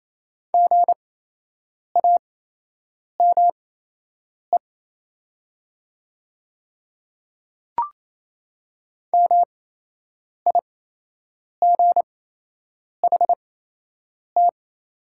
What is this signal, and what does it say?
Morse code, heard as a keyed tone, about 28 wpm, spaced as for 8 wpm, 700 Hz. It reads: GAME MIGHT
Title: 8x Character Spacing - Top 500 Words - 28wpm